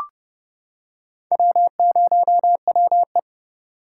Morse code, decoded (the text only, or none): W0WE